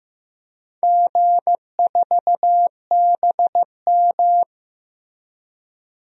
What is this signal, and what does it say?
Morse code, heard as a keyed tone, about 16 wpm, 700 Hz. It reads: G4BM
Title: DXCC-100 Call Signs - 4 Characters - Repeated 2x - 15wpm